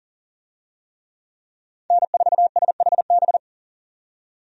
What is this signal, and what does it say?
Morse code, heard as a keyed tone, about 40 wpm, 700 Hz. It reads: N4SHB
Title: USA Call Signs - 5 Characters - 40wpm